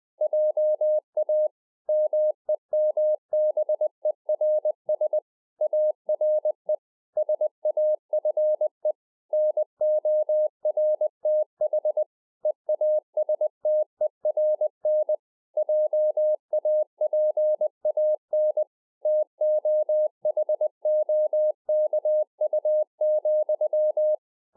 Morse code, heard as a keyed tone, about 20 wpm, 610 Hz. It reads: JA MEMBERS ARE SAFE NORTH EASTERN JAPAN TOHOKU,